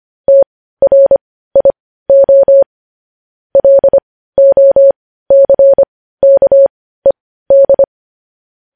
Morse code, heard as a keyed tone, about 25 wpm, spaced as for 15 wpm, 550 Hz. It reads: TRIO LOCKED
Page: news-headlines-in-morse-code-at-15-wpm